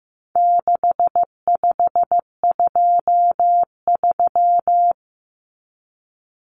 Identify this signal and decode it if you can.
Morse code, heard as a keyed tone, about 15 wpm, 700 Hz. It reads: 6523
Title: Numbers - 4-Digit 15wpm